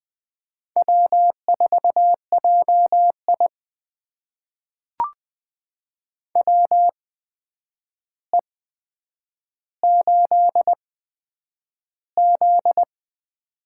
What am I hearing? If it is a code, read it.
Morse code, heard as a keyed tone, about 20 wpm, spaced as for 9 wpm, 700 Hz. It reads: W4JI WE8Z
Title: USA Call Signs - 4 Characters - 8x Character Spacing - 20wpm